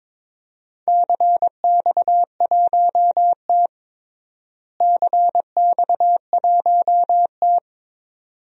Morse code, heard as a keyed tone, about 22 wpm, 700 Hz. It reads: CX1T CX1T
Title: DXCC-100 Call Signs - 4 Characters - Repeated 2x - 22wpm